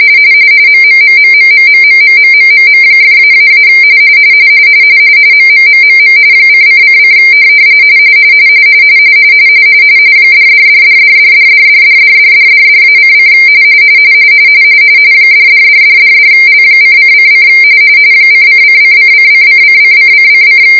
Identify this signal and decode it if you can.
RTTY, about 45.45 baud. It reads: MMMMMMMMMMMMM!--- MMMMM :::: $$$$$$$$$$$$$$$-:: $$$$$$L:::: ::::::::::::::